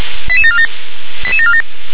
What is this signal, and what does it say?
Five-tone selective call (ZVEI), heard as quick runs of five tones, tones of about 70 ms. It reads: 88747 88747